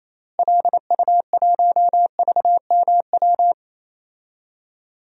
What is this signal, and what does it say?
Morse code, heard as a keyed tone, about 28 wpm, 700 Hz. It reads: LU1VMW
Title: DXCC-100 Call Signs - 6 Characters - Repeated 3x - 28wpm